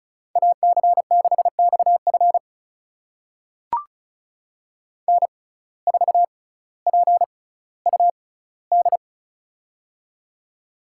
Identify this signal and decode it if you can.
Morse code, heard as a keyed tone, about 36 wpm, spaced as for 18 wpm, 700 Hz. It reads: AC6XF N4PUD
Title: USA Call Signs - 5 Characters - 6x Character Spacing - 35wpm